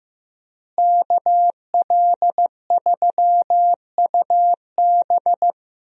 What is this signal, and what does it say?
Morse code, heard as a keyed tone, about 15 wpm, 700 Hz. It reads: KL3UB
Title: USA Call Signs - 5 Characters - Repeated 2x - 15wpm